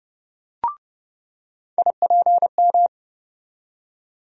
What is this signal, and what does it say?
Morse code, heard as a keyed tone, about 32 wpm, 700 Hz. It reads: IPM
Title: USA Call Sign Postfixes - 3 Letters 30wpm